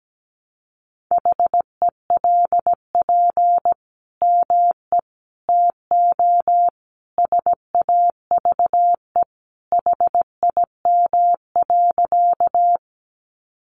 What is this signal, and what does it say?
Morse code, heard as a keyed tone, about 17 wpm, 700 Hz. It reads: HELP ME TO SAVE HIM.
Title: Sentences from Top 1100 Words 17wpm